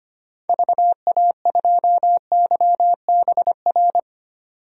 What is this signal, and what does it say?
Morse code, heard as a keyed tone, about 25 wpm, 700 Hz. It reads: VA2YBR
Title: DXCC-100 Call Signs - 6 Characters - 25wpm